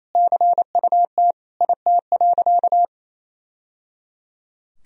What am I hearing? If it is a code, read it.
Morse code, heard as a keyed tone, about 28 wpm, 700 Hz. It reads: CUT IT.
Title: Sentences from Top 1000 Words - Review Part 1 - 28wpm